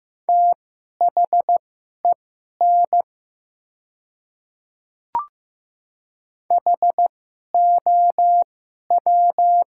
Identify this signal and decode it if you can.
Morse code, heard as a keyed tone, about 16 wpm, spaced as for 11 wpm, 700 Hz. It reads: THEN HOW
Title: 2x Character Spacing - Top 200 Words - 15wpm